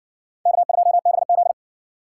Morse code, heard as a keyed tone, about 60 wpm, 700 Hz. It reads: K2BZ